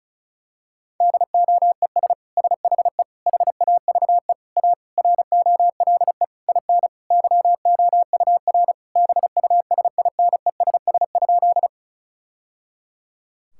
Morse code, heard as a keyed tone, about 36 wpm, 700 Hz. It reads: DOES SHE HAVE A ROLE IN YOUR BUSINESS?
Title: Sentences from Top 1000 Words - Review Part 6 - 35wpm